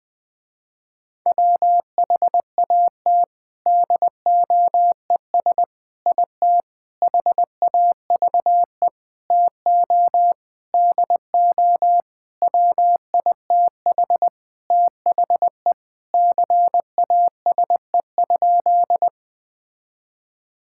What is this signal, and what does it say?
Morse code, heard as a keyed tone, about 20 wpm, 700 Hz. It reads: WHAT DOES IT HAVE TO DO WITH THE CASE?
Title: Sentences with 4-Word Phrases - Spoken Beginning - 20wpm